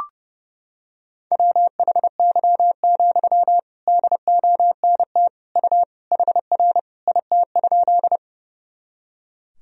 Morse code, heard as a keyed tone, about 30 wpm, 700 Hz. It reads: WHY, DONT U HR IT?
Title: Sentences from Top 300 Words and Abbreviations 30wpm